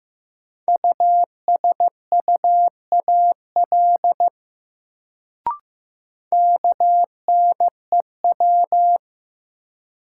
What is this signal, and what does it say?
Morse code, heard as a keyed tone, about 15 wpm, 700 Hz. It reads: USUAL KNEW
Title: Sentences from Top 400 Words - New Word Drill - 15wpm